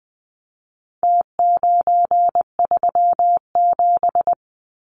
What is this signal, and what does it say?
Morse code, heard as a keyed tone, about 20 wpm, 700 Hz. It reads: T937
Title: Numbers - Contest Style - 4-Digit 20wpm